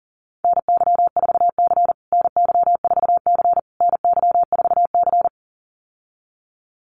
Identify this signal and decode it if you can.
Morse code, heard as a keyed tone, about 40 wpm, 700 Hz. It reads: NY4C NY4C NY4C
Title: USA Call Signs Repeated 3x 40wpm